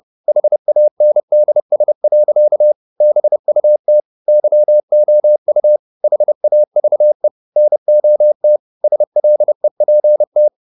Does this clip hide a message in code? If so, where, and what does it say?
Yes: Morse code from 0.3 to 10.5 s, heard as a keyed tone, about 30 wpm, 600 Hz. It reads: HANDS. BUT YOU HAVE NOT SLEPT